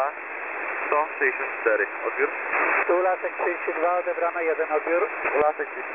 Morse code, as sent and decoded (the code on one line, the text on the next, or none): none